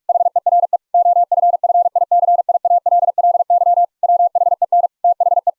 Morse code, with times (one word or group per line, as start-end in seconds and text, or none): 0.1-0.8 HERE
0.9-3.8 OFFICIALLY
4.0-4.9 WHEN
5.0-5.5 THE